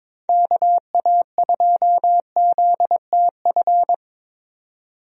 Morse code, heard as a keyed tone, about 22 wpm, 700 Hz. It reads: KA2ZTF
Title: USA Call Signs - 6 Characters - 22wpm